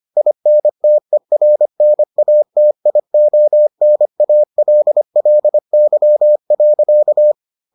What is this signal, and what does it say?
Morse code, heard as a keyed tone, about 25 wpm, 600 Hz. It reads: INTERNATIONALLY.